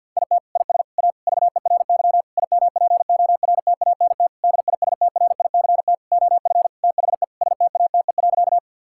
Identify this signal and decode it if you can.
Morse code, heard as a keyed tone, about 50 wpm, 700 Hz. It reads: IT IS A VERY IMPORTANT DISTRICT OF THE STATE.
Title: Sentences from Top 1600 Words 50wpm